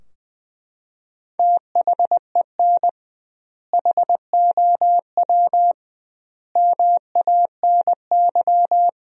Morse code, heard as a keyed tone, about 20 wpm, 700 Hz. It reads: THEN HOW MANY
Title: 2x Word-Spacing - Sentences from Top 100 Words - 20wpm